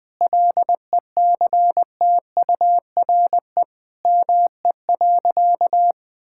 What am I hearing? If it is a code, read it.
Morse code, heard as a keyed tone, about 20 wpm, 700 Hz. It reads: LECTURE ME.
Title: Sentences from Top 1600 Words 20wpm